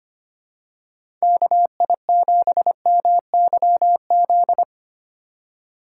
Morse code, heard as a keyed tone, about 25 wpm, 700 Hz. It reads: KI7MYZ
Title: USA Call Signs - 3x Character Spacing - 25wpm